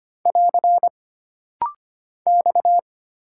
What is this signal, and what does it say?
Morse code, heard as a keyed tone, about 25 wpm, 700 Hz. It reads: <AR> X
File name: Single Character (Letter:Number:Punctuation:Prosign) - Rapid-Fire - 25wpm